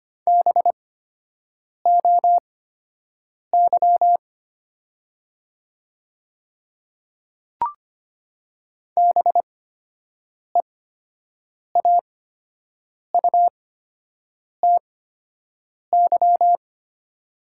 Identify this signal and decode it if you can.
Morse code, heard as a keyed tone, about 25 wpm, spaced as for 7 wpm, 700 Hz. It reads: BOY BEAUTY
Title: 8x Character Spacing - Top 500 Words - 25wpm